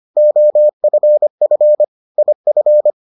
Morse code, heard as a keyed tone, about 25 wpm, 600 Hz. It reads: OFF IF